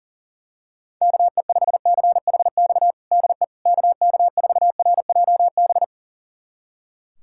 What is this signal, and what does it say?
Morse code, heard as a keyed tone, about 40 wpm, 700 Hz. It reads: KE5CHX DE KK4RJB
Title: USA Callsign DE Callsign - 6 Characters - 40wpm